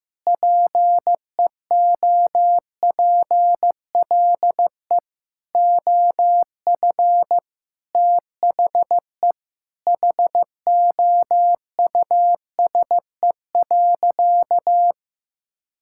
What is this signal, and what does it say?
Morse code, heard as a keyed tone, about 16 wpm, 700 Hz. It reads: PEOPLE OF THE HOUSE.
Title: Sentences from Top 1000 Words - Review Part 5 - 15wpm